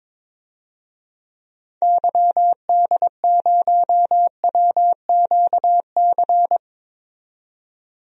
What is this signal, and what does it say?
Morse code, heard as a keyed tone, about 21 wpm, 700 Hz. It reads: YD0WQC
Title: DXCC-100 Call Signs - 6 Characters - 22wpm